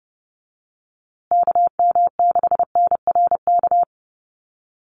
Morse code, heard as a keyed tone, about 30 wpm, 700 Hz. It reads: KM6NRK
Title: USA Call Signs Repeated 2x 30wpm